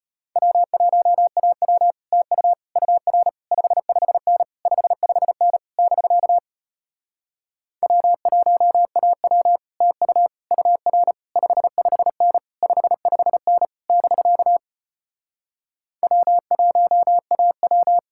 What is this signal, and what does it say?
Morse code, heard as a keyed tone, about 33 wpm, 700 Hz. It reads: W1AW TU UR 55N 55N <BK> W1AW TU UR 55N 55N <BK> W1AW